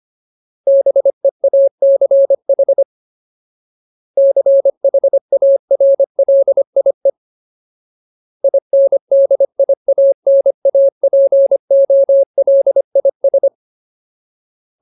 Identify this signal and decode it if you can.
Morse code, heard as a keyed tone, about 25 wpm, 550 Hz. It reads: BEACH CHARLIE INDIANAPOLIS
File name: Ultimate Word List / 25WPM Ultimate